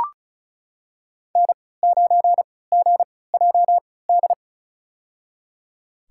Morse code, heard as a keyed tone, about 34 wpm, spaced as for 20 wpm, 690 Hz. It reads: N9GJD
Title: USA Call Signs - 5 Characters - 3x Character Spacing - 35wpm